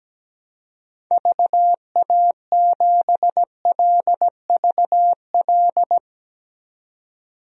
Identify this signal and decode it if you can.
Morse code, heard as a keyed tone, about 17 wpm, 700 Hz. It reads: VA7LVL
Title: DXCC-100 Call Signs - 6 Characters - 17wpm